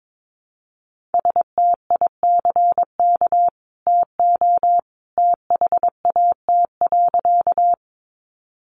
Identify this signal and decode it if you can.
Morse code, heard as a keyed tone, about 22 wpm, 700 Hz. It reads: STICK TO THAT.
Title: Sentences from Top 1200 Words 22wpm